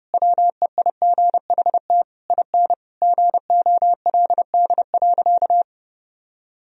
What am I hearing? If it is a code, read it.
Morse code, heard as a keyed tone, about 32 wpm, 700 Hz. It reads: WEIGHT IN GOLD.
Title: Sentences from Top 1000 Words - Review Part 3 - 30wpm